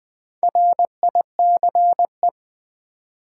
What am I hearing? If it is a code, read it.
Morse code, heard as a keyed tone, about 20 wpm, 700 Hz. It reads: RICE
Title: Build A Word - 2 Letters At A Time - 4-Letter Words - 20wpm